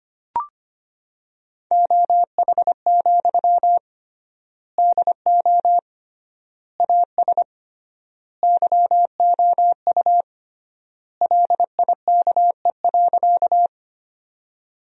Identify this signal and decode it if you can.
Morse code, heard as a keyed tone, about 25 wpm, 700 Hz. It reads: OH, DO AS YOU LIKE.
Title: 3x Word-Spacing - Sentences from Top 500 Words - 25wpm